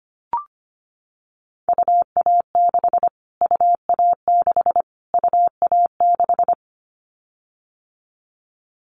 Morse code, heard as a keyed tone, about 25 wpm, 700 Hz. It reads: UA6 UA6 UA6
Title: Callsign Prefixes for Top 25 DXCC - Repeated 3x - 25wpm